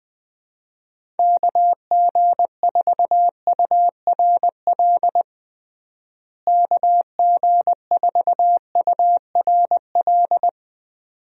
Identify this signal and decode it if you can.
Morse code, heard as a keyed tone, about 20 wpm, 700 Hz. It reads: KG4URL KG4URL
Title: USA Call Signs - 6 Characters - Repeated 2x - 20wpm